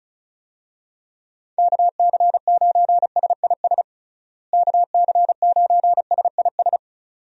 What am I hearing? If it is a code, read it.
Morse code, heard as a keyed tone, about 36 wpm, 700 Hz. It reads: KC9SIS KC9SIS
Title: USA Call Signs - 6 Characters - Repeated 2x - 35wpm